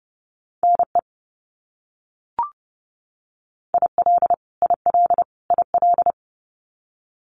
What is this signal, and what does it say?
Morse code, heard as a keyed tone, about 30 wpm, 700 Hz. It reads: NE IL IL IL
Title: US States Repeated 3x 30wpm